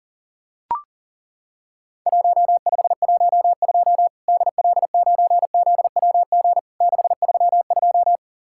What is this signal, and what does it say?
Morse code, heard as a keyed tone, about 40 wpm, 700 Hz. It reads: 1512 DL9ZWG 632
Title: Worked All Europe DX Contest - QTC Exchange - 40wpm